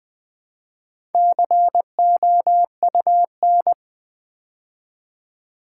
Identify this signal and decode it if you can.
Morse code, heard as a keyed tone, about 19 wpm, 700 Hz. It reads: COUN
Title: Build A Word - 4 Letters At A Time - 11-Letter Words - 20wpm